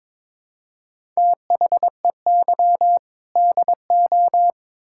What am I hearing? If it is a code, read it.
Morse code, heard as a keyed tone, about 22 wpm, 700 Hz. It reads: THEY DO
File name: Build A Phrase - 2-Word Phrases - 22wpm